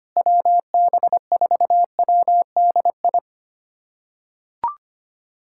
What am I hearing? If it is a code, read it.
Morse code, heard as a keyed tone, about 25 wpm, 700 Hz. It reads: WB4WDI
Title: USA Callsign DE Callsign - 6 Characters - 25wpm